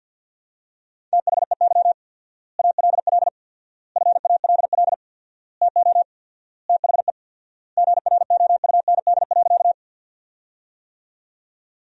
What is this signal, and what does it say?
Morse code, heard as a keyed tone, about 47 wpm, 700 Hz. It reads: THEY ALL FALL TO THE GROUND.
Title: 4x Word-Spacing - Sentences from Top 500 Words - 50wpm